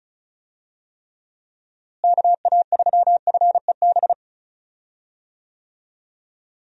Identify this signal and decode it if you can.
Morse code, heard as a keyed tone, about 36 wpm, 700 Hz. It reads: KA3FEB
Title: USA Call Signs - 6 Characters - Repeated 3x - 35wpm